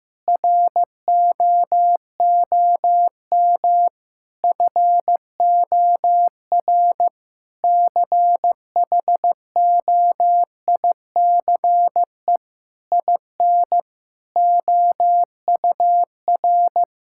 Morse code, heard as a keyed tone, about 15 wpm, 700 Hz. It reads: ROOM FOR CHOICE IN OUR
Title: Sentences from Top 1000 Words - Review Part 6 - 15wpm